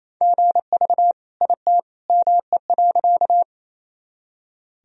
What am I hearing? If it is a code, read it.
Morse code, heard as a keyed tone, about 28 wpm, 700 Hz. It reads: GV IT ME.
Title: Sentences from Top 200 Words and Abbreviations 28wpm